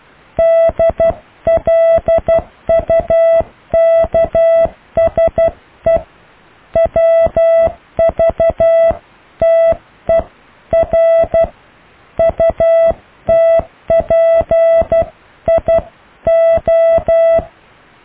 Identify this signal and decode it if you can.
Morse code, heard as a keyed tone, about 12 wpm, 660 Hz. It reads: DLUKSE WVTER UTPIO
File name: cw / IJLUM+CDRO+EKTPV+BFHNW